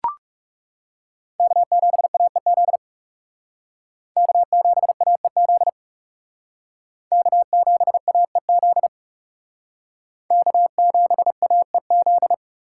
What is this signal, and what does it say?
Morse code, heard as a keyed tone, about 39 wpm, 700 Hz. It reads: K7AEZ K7AEZ K7AEZ K7AEZ